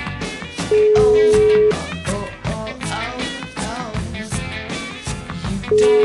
Ringback tone (Europe, 425 Hz), with noise.